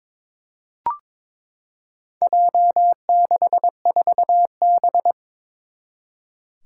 Morse code, heard as a keyed tone, about 21 wpm, 700 Hz. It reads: J64B
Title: Four Challenging Letter-Number Combinations 22wpm